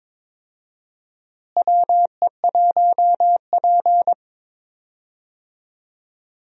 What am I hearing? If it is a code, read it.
Morse code, heard as a keyed tone, about 22 wpm, 700 Hz. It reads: WE1P
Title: USA Call Signs - 4 Characters - Repeated 3x - 22wpm